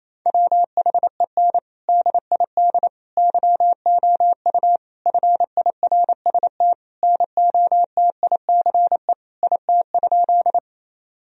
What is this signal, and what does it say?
Morse code, heard as a keyed tone, about 28 wpm, 700 Hz. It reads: WHEN DID YOU FIRST NOTICE IT?
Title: Sentences from Top 1000 Words - Review Part 2 - 28wpm